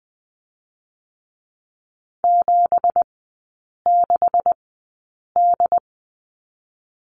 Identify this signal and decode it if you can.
Morse code, heard as a keyed tone, about 20 wpm, spaced as for 8 wpm, 690 Hz. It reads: 76D
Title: 212 Lesson 34e - Number 8 - Three Characters 20wpm